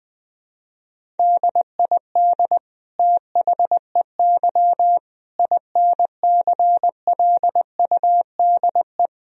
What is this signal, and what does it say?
Morse code, heard as a keyed tone, about 20 wpm, 700 Hz. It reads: DID THEY INCLUDE
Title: Sentences from Top 1000 Words - Review Part 3 - 20wpm